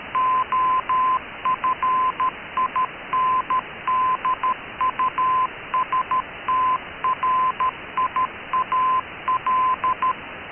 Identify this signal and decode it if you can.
Morse code, heard as a keyed tone, about 13 wpm, 1030 Hz, with noise. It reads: OF INDUSTRIAL
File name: KKS-1 cw 21jul2014 1425z